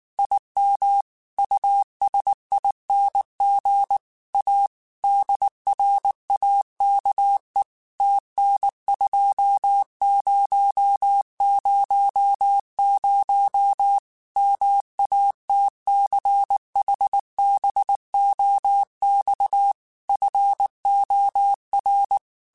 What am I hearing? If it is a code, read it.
Morse code, heard as a keyed tone, about 19 wpm, 800 Hz. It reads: IM USING A DRAKE TN2000 MATCHBOX FOR
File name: QSOs / QSOs-mp3s / WD0FBT 1@18